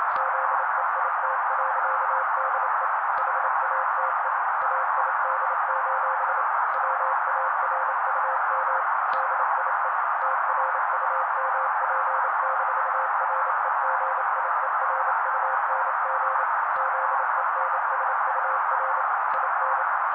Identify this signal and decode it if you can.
Morse code, heard as a keyed tone, about 28 wpm, 530 Hz. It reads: ZEIT.DE HATE AID: WARUM DIE TRUMP-REGIERUNG ZENSUR IN
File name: daily / 27wpm noise